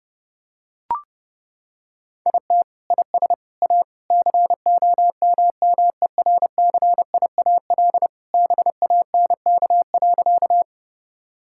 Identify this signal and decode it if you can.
Morse code, heard as a keyed tone, about 30 wpm, 700 Hz. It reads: IT IS A COMMERCIAL BANK.
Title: Sentences from Top 1000 Words - Review Part 6 - 30wpm